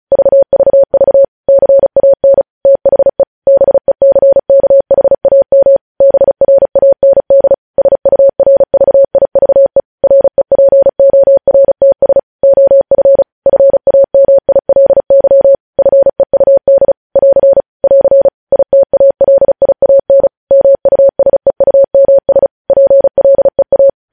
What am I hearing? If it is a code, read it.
Morse code, heard as a keyed tone, about 36 wpm, 550 Hz. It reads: VVV CAN THE BECKHAM BRAND SURVIVE REPORTS OF FAMILY FEUD <AR> <AR> ITALIAN MUSEUMS PLEA
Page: news headlines in morse code at 35 wpm